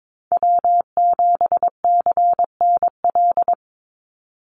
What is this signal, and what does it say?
Morse code, heard as a keyed tone, about 22 wpm, 700 Hz. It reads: W7CNL
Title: NA SOTA Chaser Callsigns 22wpm